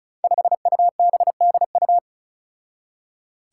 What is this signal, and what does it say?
Morse code, heard as a keyed tone, about 36 wpm, 700 Hz. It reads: 5UBDU
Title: Five Challenging Letter-Number Combinations 35wpm